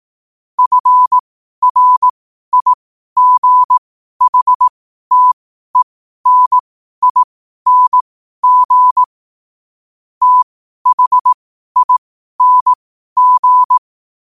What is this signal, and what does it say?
Morse code, heard as a keyed tone, about 18 wpm, spaced as for 12 wpm, 1000 Hz. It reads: FRIGHTENING THING
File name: qotdfiles / 12wpm